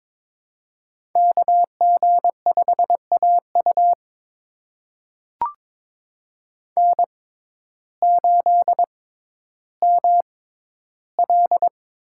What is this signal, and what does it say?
Morse code, heard as a keyed tone, about 22 wpm, spaced as for 12 wpm, 700 Hz. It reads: KG5AU N8ML
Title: USA Call Signs - 5 Characters - 6x Character Spacing - 22wpm